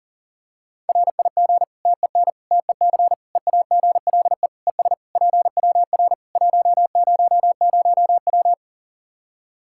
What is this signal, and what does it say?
Morse code, heard as a keyed tone, about 37 wpm, 700 Hz. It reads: RIG TEN TEC EAGLE ES PWR 100W
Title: Practice QSO - Exchange 2 - Info - 40wpm